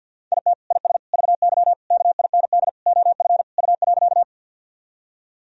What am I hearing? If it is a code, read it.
Morse code, heard as a keyed tone, about 46 wpm, 700 Hz. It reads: IT IS VY KIND OF U.
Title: Sentences from Top 200 Words and Abbreviations 50wpm